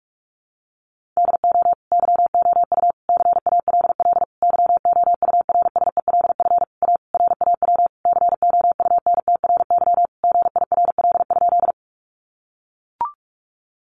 Morse code, heard as a keyed tone, about 44 wpm, 700 Hz. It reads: DO YOU CALL YOURSELF A RAW COUNTRY GIRL?
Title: Sentences from Top 1500 Words 45wpm